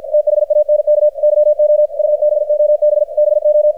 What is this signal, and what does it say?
Morse code, heard as a keyed tone, about 50 wpm, 600 Hz, with radio noise. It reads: THANK YOU BOB BO
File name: archived / CRSnet